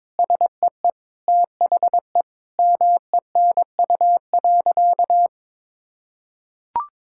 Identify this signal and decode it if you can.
Morse code, heard as a keyed tone, about 22 wpm, 700 Hz. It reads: SEE THE MENU.
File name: practice / 22wpm / Sentences with 4-Word Phrases - 22wpm